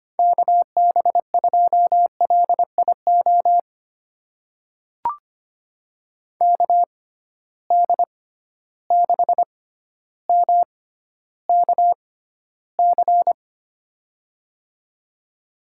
Morse code, heard as a keyed tone, about 25 wpm, spaced as for 13 wpm, 700 Hz. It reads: KB2LIO KD6MKC
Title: USA Call Signs - 6 Characters - 6x Character Spacing - 25wpm